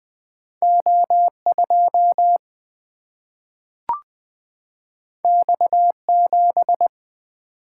Morse code, heard as a keyed tone, about 20 wpm, 700 Hz. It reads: O 2 X 7
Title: Single-Letter Single-Number 20wpm